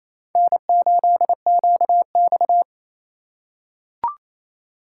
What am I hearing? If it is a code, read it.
Morse code, heard as a keyed tone, about 28 wpm, 700 Hz. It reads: N8QX
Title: USA Call Signs - 4 Characters - 2x Character Spacing - 28wpm